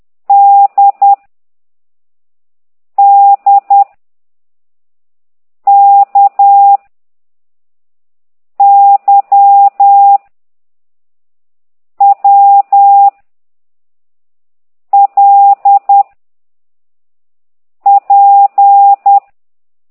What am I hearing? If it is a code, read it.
Morse code, heard as a keyed tone, about 10 wpm, 800 Hz. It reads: D D K Y W L P
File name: ct2idw / exerc9